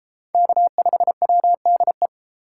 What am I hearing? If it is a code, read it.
Morse code, heard as a keyed tone, about 31 wpm, 700 Hz. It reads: K5WDE